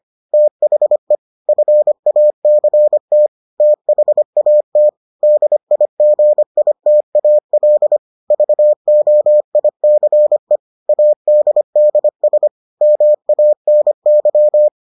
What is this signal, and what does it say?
Morse code, heard as a keyed tone, about 25 wpm, 600 Hz. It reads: THE FACT THAT DIGITAL VOICE ADDS MANY